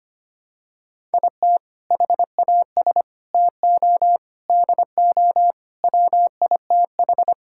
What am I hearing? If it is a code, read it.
Morse code, heard as a keyed tone, about 25 wpm, 700 Hz. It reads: IT HAS TO DO WITH
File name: practice / 25wpm / Sentences with 4-Word Phrases - 25wpm